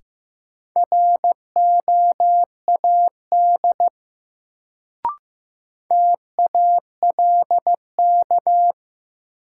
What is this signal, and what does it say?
Morse code, heard as a keyed tone, about 15 wpm, 700 Hz. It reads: ROAD TALK
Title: Sentences from Top 400 Words - New Word Drill - 15wpm